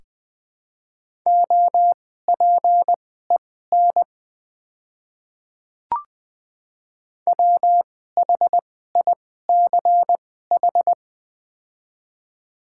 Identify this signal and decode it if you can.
Morse code, heard as a keyed tone, about 20 wpm, spaced as for 14 wpm, 700 Hz. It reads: OPEN WHICH
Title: 2x Character Spacing - Top 300 Words - 20wpm